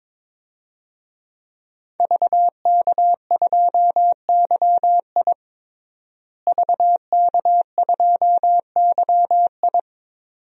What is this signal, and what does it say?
Morse code, heard as a keyed tone, about 22 wpm, 700 Hz. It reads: VK2YI VK2YI
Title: DXCC-100 Call Signs - 5 Characters - Repeated 2x - 22wpm